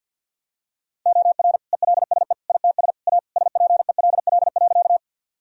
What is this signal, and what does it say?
Morse code, heard as a keyed tone, about 51 wpm, 700 Hz. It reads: OR ELSE ITS A SPELL.
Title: Sentences from Top 1000 Words - Review Part 1 - 50wpm